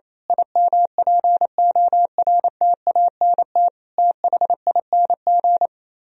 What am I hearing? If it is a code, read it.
Morse code, heard as a keyed tone, about 27 wpm, 700 Hz. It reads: IMPORTANT THING
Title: Speed-Racing - Sets of 4 Words 28wpm